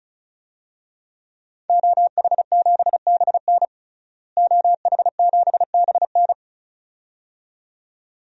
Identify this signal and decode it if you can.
Morse code, heard as a keyed tone, about 36 wpm, 700 Hz. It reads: OH7BN OH7BN
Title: DXCC-100 Call Signs - 5 Characters - Repeated 2x - 35wpm